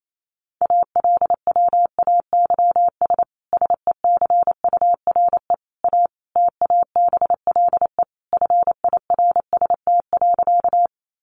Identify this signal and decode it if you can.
Morse code, heard as a keyed tone, about 28 wpm, 700 Hz. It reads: ALWAYS SECURE A TABLE FIRST.
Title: Sentences from Top 1200 Words 28wpm